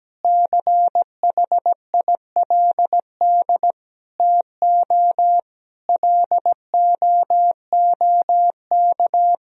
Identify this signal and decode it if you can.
Morse code, heard as a keyed tone, about 17 wpm, 700 Hz. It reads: CHILD TO LOOK
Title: Sentences from Top 1000 Words - Review Part 3 - 17wpm